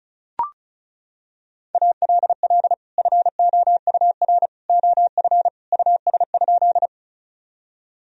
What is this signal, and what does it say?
Morse code, heard as a keyed tone, about 34 wpm, 700 Hz. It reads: ALL FOUR OF US?
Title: Sentences from Top 1000 Words - Review Part 1 - 35wpm